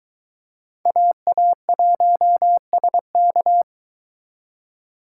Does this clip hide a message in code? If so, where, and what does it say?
Yes: Morse code from 0.9 to 3.6 s, heard as a keyed tone, about 23 wpm, 700 Hz. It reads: AA1SK